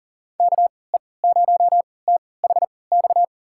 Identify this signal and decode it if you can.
Morse code, heard as a keyed tone, about 40 wpm, spaced as for 23 wpm, 700 Hz. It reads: KE0THX
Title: USA Call Signs - 3x Character Spacing - 40wpm